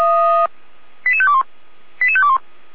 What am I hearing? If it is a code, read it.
Five-tone selective call (ZVEI), heard as quick runs of five tones, tones of about 70 ms. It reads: 88521 88521